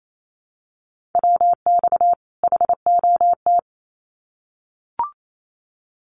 Morse code, heard as a keyed tone, about 28 wpm, 700 Hz. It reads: WX HOT
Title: Common QSO Phrases 28wpm